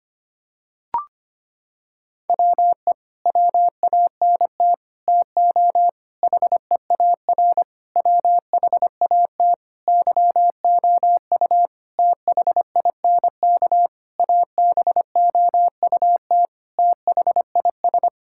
Morse code, heard as a keyed tone, about 25 wpm, 700 Hz. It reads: WE WANT TO HEAR WHAT YOU THINK ABOUT THIS
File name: Sets of 9 Words - 25wpm